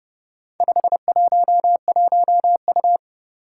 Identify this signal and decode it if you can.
Morse code, heard as a keyed tone, about 30 wpm, 700 Hz. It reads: 511U
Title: Four Challenging Letter-Number Combinations 30wpm